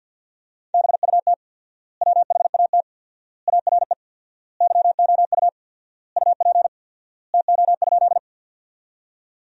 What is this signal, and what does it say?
Morse code, heard as a keyed tone, about 50 wpm, 700 Hz. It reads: BUT WHAT ARE YOU UP TO?
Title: 4x Word-Spacing - Sentences from Top 100 Words - 50wpm